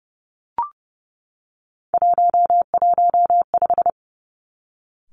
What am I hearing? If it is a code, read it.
Morse code, heard as a keyed tone, about 30 wpm, 700 Hz. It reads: 115
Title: Three Challenging Letter-Number Combinations 30wpm